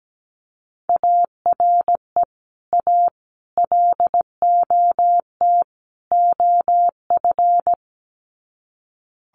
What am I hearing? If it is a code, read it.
Morse code, heard as a keyed tone, about 17 wpm, 700 Hz. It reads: ARE A LOT OF
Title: Sets of 4 Words 17wpm